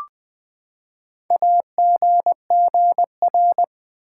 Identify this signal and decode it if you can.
Morse code, heard as a keyed tone, about 20 wpm, 700 Hz. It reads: AGGR
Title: Build A Word - 4 Letters At A Time - 10-Letter Words - 20wpm